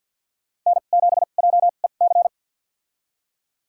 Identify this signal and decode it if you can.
Morse code, heard as a keyed tone, about 47 wpm, spaced as for 35 wpm, 690 Hz. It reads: N7JEC
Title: USA Call Signs - 5 Characters - 2x Character Spacing - 50wpm